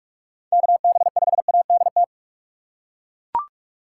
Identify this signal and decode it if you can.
Morse code, heard as a keyed tone, about 40 wpm, 700 Hz. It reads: KB5ADT